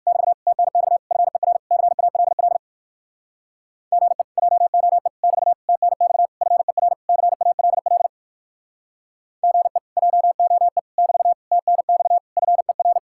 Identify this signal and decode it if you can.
Morse code, heard as a keyed tone, about 59 wpm, 700 Hz. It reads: <BT> TNX FER CALL GE JOE <BT> TNX FER CALL GE JOE <BT> TNX FER